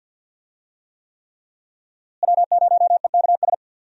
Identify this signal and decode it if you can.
Morse code, heard as a keyed tone, about 51 wpm, 700 Hz. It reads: W0EKS
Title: USA Call Signs - 5 Characters - Repeated 3x - 50wpm